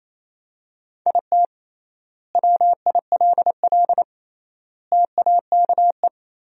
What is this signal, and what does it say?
Morse code, heard as a keyed tone, about 28 wpm, 700 Hz. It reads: IT WILL TAKE
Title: 3x Word-Spacing - Sentences from Top 200 Words - 28wpm